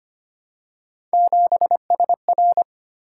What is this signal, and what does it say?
Morse code, heard as a keyed tone, about 25 wpm, 700 Hz. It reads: 7SR
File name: Three Letter-Number Combinations - 25wpm